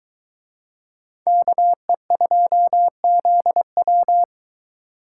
Morse code, heard as a keyed tone, about 23 wpm, 700 Hz. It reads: KE2ZW